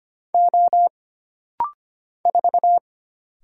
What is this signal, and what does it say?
Morse code, heard as a keyed tone, about 25 wpm, 700 Hz. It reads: O 4
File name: Single Character (Letter:Number:Punctuation:Prosign) - Rapid-Fire - 25wpm